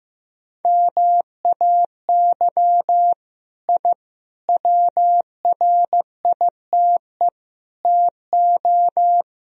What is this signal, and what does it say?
Morse code, heard as a keyed tone, about 15 wpm, 700 Hz. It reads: MAY I WRITE TO
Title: Sentences from Top 1000 Words - Review Part 2 - 15wpm